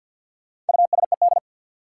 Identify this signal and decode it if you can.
Morse code, heard as a keyed tone, about 49 wpm, 700 Hz. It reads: USED
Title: Sets of 2 Words - New Word Drill - 50wpm